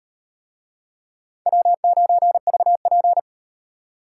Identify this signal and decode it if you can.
Morse code, heard as a keyed tone, about 36 wpm, 700 Hz. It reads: W9VP